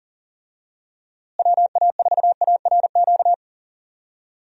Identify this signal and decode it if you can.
Morse code, heard as a keyed tone, about 40 wpm, 700 Hz. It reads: WA4ARQ
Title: USA Call Signs - 6 Characters - Repeated 3x - 40wpm